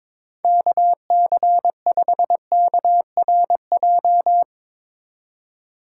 Morse code, heard as a keyed tone, about 22 wpm, 700 Hz. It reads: KC5KRJ
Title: USA Call Signs - 6 Characters - 2x Character Spacing - 22wpm